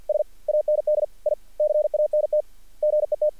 Morse code, heard as a keyed tone, about 48 wpm, 600 Hz. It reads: S AND I CANT GET
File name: archived / CRSnet